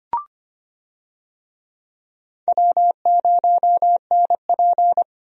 Morse code, heard as a keyed tone, about 25 wpm, 700 Hz. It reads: W0NP
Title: USA Call Signs - 4 Characters - Repeated 2x - 25wpm